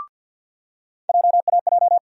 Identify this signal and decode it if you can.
Morse code, heard as a keyed tone, about 48 wpm, 690 Hz. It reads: JAJ
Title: USA Call Sign Postfixes - 3 Letters 50wpm